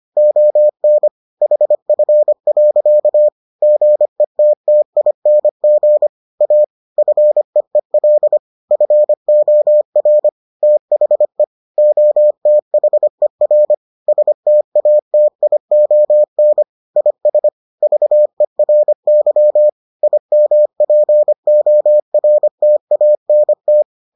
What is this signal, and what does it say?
Morse code, heard as a keyed tone, about 25 wpm, 600 Hz. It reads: ON HF. GETTING A FEEL FOR THE OTHER STATION IS VERY IMPORTANT